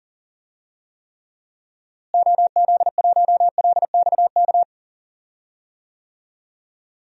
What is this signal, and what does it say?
Morse code, heard as a keyed tone, about 37 wpm, 700 Hz. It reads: OZ1LXK
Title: DXCC-100 Call Signs - 6 Characters - Repeated 2x - 40wpm